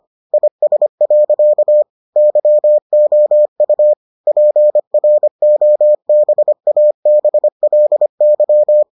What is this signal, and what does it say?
Morse code, heard as a keyed tone, about 25 wpm, 600 Hz. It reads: IS. YOU PROBABLY